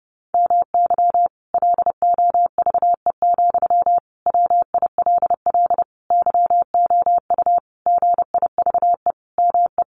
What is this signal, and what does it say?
Morse code, heard as a keyed tone, about 28 wpm, 700 Hz. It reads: MY LOVE, WILL YOU GIVE ME
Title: Sentences from Top 1200 Words 30wpm